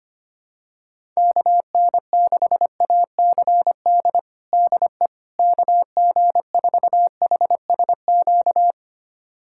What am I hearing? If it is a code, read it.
Morse code, heard as a keyed tone, about 25 wpm, 700 Hz. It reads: KN6ACD DE KG4HSQ
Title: USA Callsign DE Callsign 25wpm